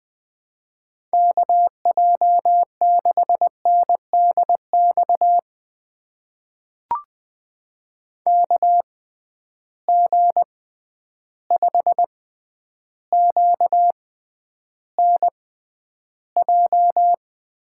Morse code, heard as a keyed tone, about 20 wpm, spaced as for 10 wpm, 700 Hz. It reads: KJ6NDX KG5QNJ
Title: USA Call Signs - 6 Characters - 6x Character Spacing - 20wpm